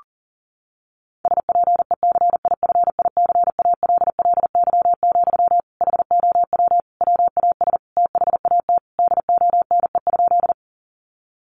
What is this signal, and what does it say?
Morse code, heard as a keyed tone, about 40 wpm, 700 Hz. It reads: SPECIFICALLY, HOW WAS THAT DONE?
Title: Sentences from Top 1100 Words 40wpm